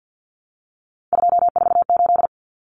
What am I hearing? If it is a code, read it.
Morse code, heard as a keyed tone, about 50 wpm, 700 Hz. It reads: 248
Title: Numbers - 3-Digit 50wpm